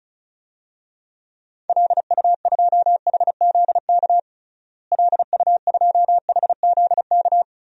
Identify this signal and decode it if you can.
Morse code, heard as a keyed tone, about 36 wpm, 700 Hz. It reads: LU2HZK LU2HZK
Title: DXCC-100 Call Signs - 6 Characters - Repeated 3x - 35wpm